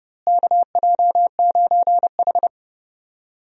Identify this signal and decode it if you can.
Morse code, heard as a keyed tone, about 31 wpm, 700 Hz. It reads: KJ9H
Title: USA Call Signs - 4 Characters - Repeated 2x - 30wpm